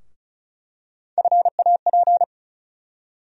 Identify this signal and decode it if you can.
Morse code, heard as a keyed tone, about 34 wpm, 700 Hz. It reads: FAP
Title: USA Call Sign Postfixes - 3 Letters 35wpm